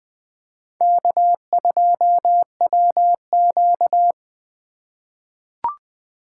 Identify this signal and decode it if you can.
Morse code, heard as a keyed tone, about 20 wpm, 700 Hz. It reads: K2WQ
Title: USA Call Signs - 4 Characters - 2x Character Spacing - 20wpm